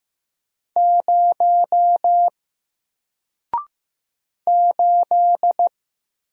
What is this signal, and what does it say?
Morse code, heard as a keyed tone, about 15 wpm, 700 Hz. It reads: 0 8
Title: Single Letter-Number Alt 15wpm